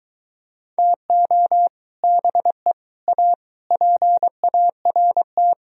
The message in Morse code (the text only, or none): TO BE A PART